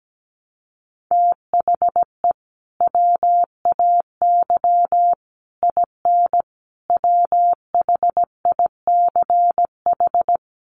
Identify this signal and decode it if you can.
Morse code, heard as a keyed tone, about 17 wpm, 700 Hz. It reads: THE WAY IN WHICH
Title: Sets of 5 Words Encore2 17wpm